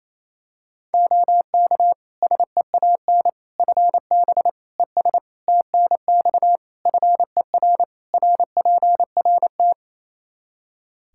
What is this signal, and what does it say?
Morse code, heard as a keyed tone, about 27 wpm, 700 Hz. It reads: OK SEAN FB ES TNX FER RPRT
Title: Practice QSO - Exchange 2 - Info - 28wpm